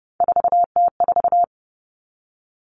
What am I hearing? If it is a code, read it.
Morse code, heard as a keyed tone, about 30 wpm, 700 Hz. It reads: 4T4
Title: Common Cut Numbers - 3-Digit 30wpm